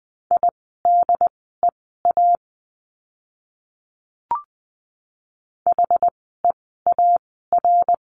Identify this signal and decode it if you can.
Morse code, heard as a keyed tone, about 20 wpm, spaced as for 15 wpm, 700 Hz. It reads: IDEA HEAR
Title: 108 Lesson 18f - Letter m - Words 20wpm